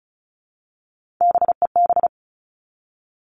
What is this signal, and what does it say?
Morse code, heard as a keyed tone, about 34 wpm, 700 Hz. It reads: BEB
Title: All Cut Numbers - 3-Digit 35wpm